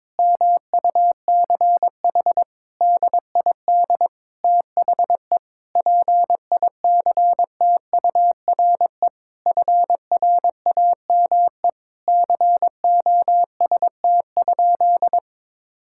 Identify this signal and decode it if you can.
Morse code, heard as a keyed tone, about 22 wpm, 700 Hz. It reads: MUCH DID THE PICTURE FRAME COST?
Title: Sentences from Top 1000 Words - Review Part 6 - 22wpm